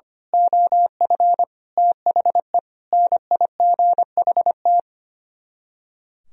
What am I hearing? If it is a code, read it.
Morse code, heard as a keyed tone, about 24 wpm, 700 Hz. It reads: OF THE NIGHT
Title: Speed-Racing - Sets of 5 Words 25wpm